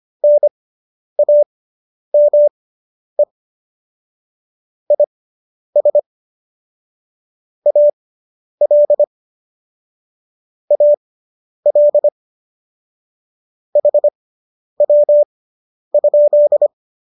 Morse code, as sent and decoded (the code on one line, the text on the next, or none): -. .- -- . / .. ... / .- .-.. / .- .-.. / .... .-- ..--..
NAME IS AL AL HW?